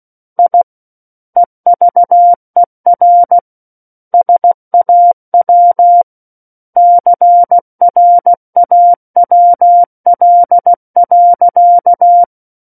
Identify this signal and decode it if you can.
Morse code, heard as a keyed tone, about 16 wpm, 700 Hz. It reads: I EVER SAW CRAWL.
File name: Book One / 07-Chapter 7 - 16 WPM